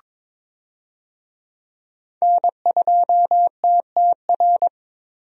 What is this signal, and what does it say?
Morse code, heard as a keyed tone, about 21 wpm, 700 Hz. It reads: N2TTR
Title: USA Call Signs - 5 Characters - Repeated 2x - 22wpm